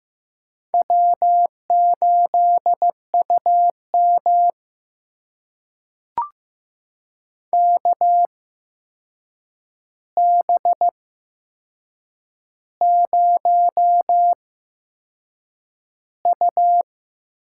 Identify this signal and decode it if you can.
Morse code, heard as a keyed tone, about 15 wpm, spaced as for 6 wpm, 700 Hz. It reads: W8UM KB0U
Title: USA Call Signs - 4 Characters - 8x Character Spacing - 15wpm